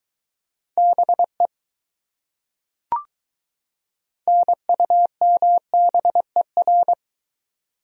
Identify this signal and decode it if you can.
Morse code, heard as a keyed tone, about 22 wpm, 700 Hz. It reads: BE NUMBER